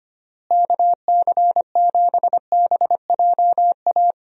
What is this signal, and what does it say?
Morse code, heard as a keyed tone, about 25 wpm, 700 Hz. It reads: KC7BJA
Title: USA Call Signs - 6 Characters - Repeated 3x - 25wpm